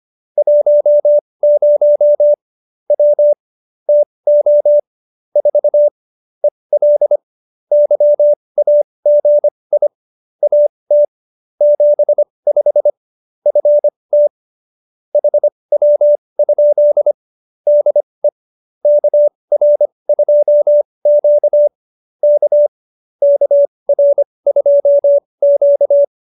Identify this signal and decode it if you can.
Morse code, heard as a keyed tone, about 25 wpm, spaced as for 20 wpm, 590 Hz. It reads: 10 W TO 4 EL YAGI AT 75 FT HW? DE KR2Q K KR2Q
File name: qso220b 20